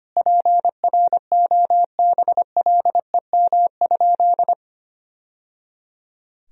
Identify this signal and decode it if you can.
Morse code, heard as a keyed tone, about 25 wpm, 700 Hz. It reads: PROBLEM?
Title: Sentences from Top 1000 Words - Review Part 4 - 25wpm